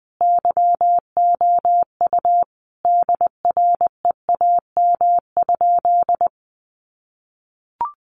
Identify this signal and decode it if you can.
Morse code, heard as a keyed tone, about 20 wpm, 700 Hz. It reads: YOU DREAM?
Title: Sentences from Top 1500 Words 20wpm